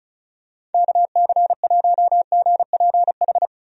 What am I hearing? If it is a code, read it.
Morse code, heard as a keyed tone, about 34 wpm, 700 Hz. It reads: KC1GPH
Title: USA Call Signs - 6 Characters - Repeated 3x - 35wpm